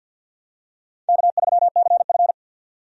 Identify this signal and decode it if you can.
Morse code, heard as a keyed tone, about 52 wpm, 700 Hz. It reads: K3CF
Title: USA Call Signs - 4 Characters - 50wpm